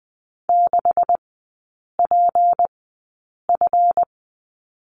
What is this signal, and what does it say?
Morse code, heard as a keyed tone, about 20 wpm, spaced as for 8 wpm, 700 Hz. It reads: 6PF
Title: 152 Lesson 25e - Letter p - Three Characters 20wpm